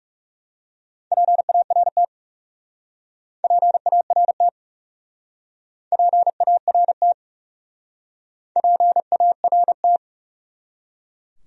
Morse code, heard as a keyed tone, about 39 wpm, 700 Hz. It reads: PART PART PART PART